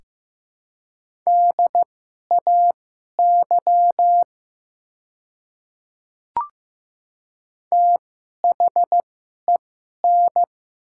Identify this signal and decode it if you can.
Morse code, heard as a keyed tone, about 15 wpm, spaced as for 11 wpm, 700 Hz. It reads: DAY THEN
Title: 2x Character Spacing - Top 100 Words - 15wpm